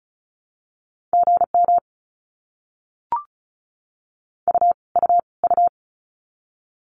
Morse code, heard as a keyed tone, about 34 wpm, 700 Hz. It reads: GM U U U
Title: Abbreviations Repeated 3x - 35wpm